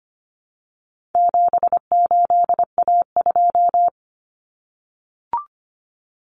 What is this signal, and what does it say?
Morse code, heard as a keyed tone, about 25 wpm, 700 Hz. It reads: 78A2
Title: Common Cut Numbers - 4-Digit 25wpm